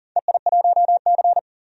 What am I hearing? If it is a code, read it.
Morse code, heard as a keyed tone, about 40 wpm, 700 Hz. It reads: EI1C
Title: DXCC-100 Call Signs - 4 Characters - 40wpm